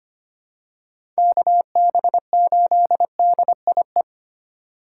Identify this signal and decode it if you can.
Morse code, heard as a keyed tone, about 25 wpm, 700 Hz. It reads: KB8DIE
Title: USA Call Signs - 6 Characters - 4x Character Spacing - 25wpm